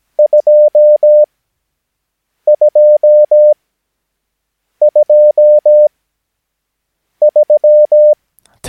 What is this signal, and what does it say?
Morse code, heard as a keyed tone, about 17 wpm, spaced as for 6 wpm, 600 Hz. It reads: 2223
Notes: Övningsmaterial med alfabetet A till Ö samt siffror 0 till 9 enligt International/Continental Morse Code.